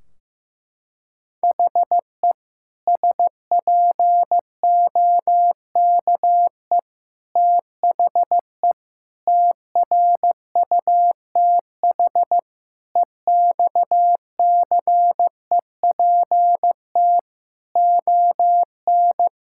Morse code, heard as a keyed tone, about 15 wpm, 700 Hz. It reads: HE SPOKE THE TRUTH EXCEPT ON
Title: Sentences from Top 1600 Words 15wpm